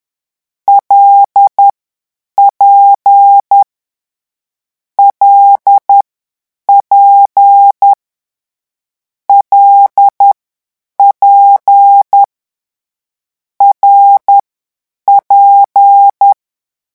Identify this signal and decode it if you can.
Morse code, heard as a keyed tone, about 10 wpm, spaced as for 8 wpm, 800 Hz. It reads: LP LP LP RP
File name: morse mp3 / lecon6-vitesse 10